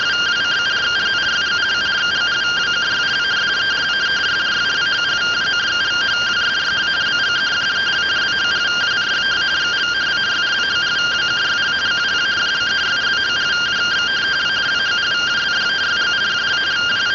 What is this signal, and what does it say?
Two-tone teleprinter signal, about 170 Hz shift.
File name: findings / CT 205 ax mips, iDose 3 0006.gen-44 rtty rx